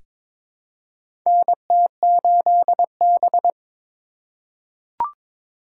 Morse code, heard as a keyed tone, about 22 wpm, 700 Hz. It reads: NT8B
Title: USA Call Signs - 4 Characters - 4x Character Spacing - 22wpm